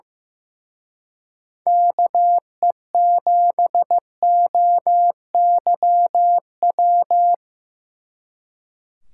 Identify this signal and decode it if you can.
Morse code, heard as a keyed tone, about 18 wpm, 700 Hz. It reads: KE7OYW
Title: Speed Racing - USA 2x3 Call Signs 15wpm